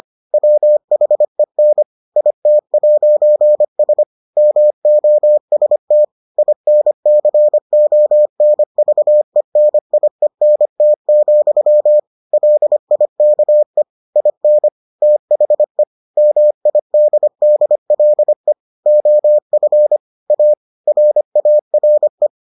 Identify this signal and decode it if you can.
Morse code, heard as a keyed tone, about 25 wpm, 600 Hz. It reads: WHEN IT'S MOST INCONVENIENT, LIKE IN THE MIDDLE OF A RARE